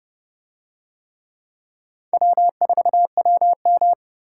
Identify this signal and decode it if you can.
Morse code, heard as a keyed tone, about 29 wpm, 700 Hz. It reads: W4WM